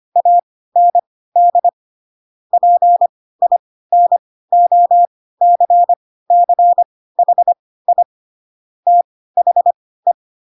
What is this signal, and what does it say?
Morse code, heard as a keyed tone, about 25 wpm, spaced as for 16 wpm, 700 Hz. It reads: AND PINOCCHI THE
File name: Pinocchio / Pinocchio ch3 10 25 ik1hge